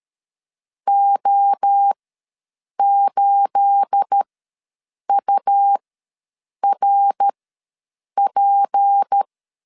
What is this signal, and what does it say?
Morse code, heard as a keyed tone, about 13 wpm, spaced as for 7 wpm, 800 Hz. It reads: O8URP
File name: LEZIONI / 9 28 ws 12wpm 3354